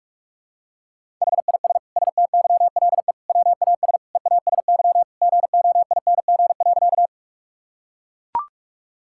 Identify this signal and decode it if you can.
Morse code, heard as a keyed tone, about 41 wpm, 700 Hz. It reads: HIS STYLE WAS EASY GOING.
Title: Sentences from Top 1000 Words - Review Part 5 - 45wpm